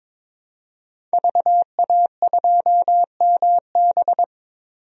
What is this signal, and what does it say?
Morse code, heard as a keyed tone, about 22 wpm, 700 Hz. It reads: VA2MB
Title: DXCC-100 Call Signs - 5 Characters - Repeated 3x - 22wpm